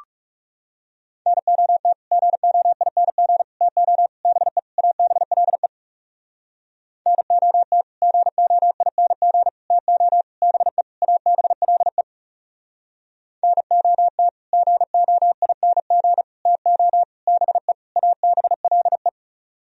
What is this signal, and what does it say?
Morse code, heard as a keyed tone, about 39 wpm, 700 Hz. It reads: NOT GOING TO BE ABLE NOT GOING TO BE ABLE NOT GOING TO BE ABLE